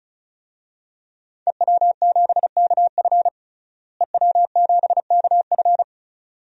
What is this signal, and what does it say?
Morse code, heard as a keyed tone, about 36 wpm, 700 Hz. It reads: EW7KF EW7KF
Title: DXCC-100 Call Signs - 5 Characters - Repeated 2x - 35wpm